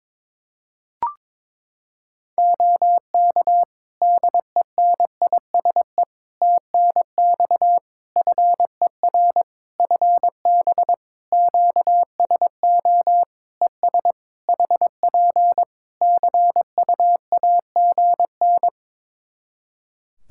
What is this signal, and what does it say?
Morse code, heard as a keyed tone, about 22 wpm, 700 Hz. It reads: OK DENISE TNX FER FB QSO ES HP CUAGN
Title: Practice QSO - Exchange 4 - Ending - 22wpm